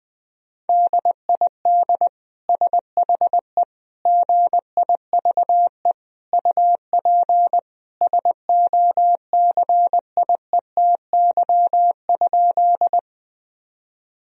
Morse code, heard as a keyed tone, about 19 wpm, 700 Hz. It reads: DID SHE GIVE UP SOCIETY?
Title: Sentences from Top 1000 Words - Review Part 4 - 20wpm